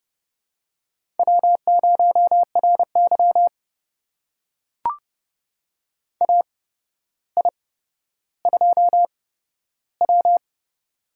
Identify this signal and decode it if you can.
Morse code, heard as a keyed tone, about 31 wpm, spaced as for 13 wpm, 700 Hz. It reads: W0RY AI2W
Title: USA Call Signs - 4 Characters - 8x Character Spacing - 30wpm